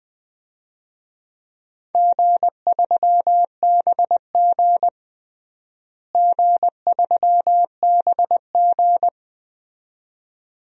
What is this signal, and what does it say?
Morse code, heard as a keyed tone, about 20 wpm, 700 Hz. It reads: G3BG G3BG
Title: DXCC-100 Call Signs - 4 Characters - Repeated 2x - 20wpm